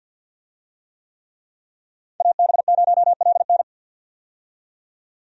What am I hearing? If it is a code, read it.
Morse code, heard as a keyed tone, about 48 wpm, 700 Hz. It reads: AB0LN
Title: USA Call Signs - 5 Characters - 50wpm